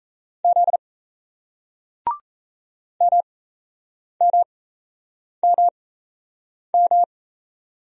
Morse code, heard as a keyed tone, about 34 wpm, spaced as for 8 wpm, 700 Hz. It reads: Z MMMM
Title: Speed-Racing Single Letters 28wpm